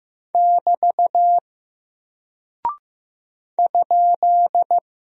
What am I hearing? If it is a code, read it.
Morse code, heard as a keyed tone, about 15 wpm, 700 Hz. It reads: <BT> ?
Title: Single Character (Letter:Number:Punctuation:Prosign) Alt 15wpm